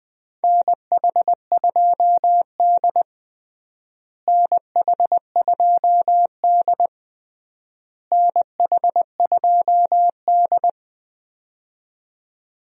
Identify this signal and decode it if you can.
Morse code, heard as a keyed tone, about 21 wpm, 700 Hz. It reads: NH2D NH2D NH2D
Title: USA Call Signs - 4 Characters - Repeated 3x - 20wpm